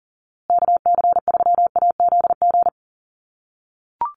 Morse code, heard as a keyed tone, about 40 wpm, 700 Hz. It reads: KC3AZG
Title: USA Call Signs Repeated 2x 40wpm